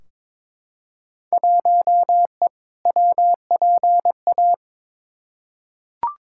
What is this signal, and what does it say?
Morse code, heard as a keyed tone, about 22 wpm, 700 Hz. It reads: 1E WPA
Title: Field Day Exchanges - 22wpm